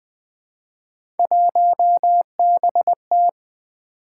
Morse code, heard as a keyed tone, about 20 wpm, 700 Hz. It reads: 1BT
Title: Three Letter-Number Combinations - 20wpm